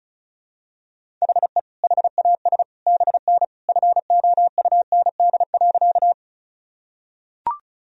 Morse code, heard as a keyed tone, about 36 wpm, 700 Hz. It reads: HE HAS BN FOUND.
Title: Sentences from Top 300 Words and Abbreviations 35wpm